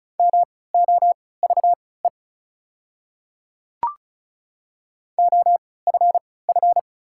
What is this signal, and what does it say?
Morse code, heard as a keyed tone, about 36 wpm, spaced as for 20 wpm, 690 Hz. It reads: MOVE OFF
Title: 3x Character Spacing - Top 200 Words - 35wpm